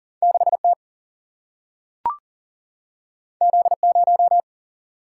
Morse code, heard as a keyed tone, about 40 wpm, 700 Hz. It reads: 6 T Z 0
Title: Single-Letter Single-Number 40wpm